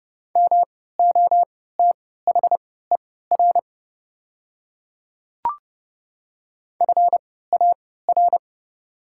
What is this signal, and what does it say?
Morse code, heard as a keyed tone, about 30 wpm, spaced as for 17 wpm, 690 Hz. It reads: MOTHER FAR
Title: 3x Character Spacing - Top 300 Words - 30wpm